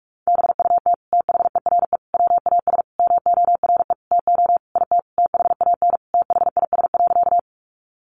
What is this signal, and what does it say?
Morse code, heard as a keyed tone, about 44 wpm, 700 Hz. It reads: BUT THERE WAS MORE TO IT THAN THIS.
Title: Sentences from Top 100 Words 45wpm